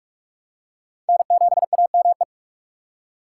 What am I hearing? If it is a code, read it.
Morse code, heard as a keyed tone, about 41 wpm, 700 Hz. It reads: N7AME
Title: K1USN SST QSO Practice - 45wpm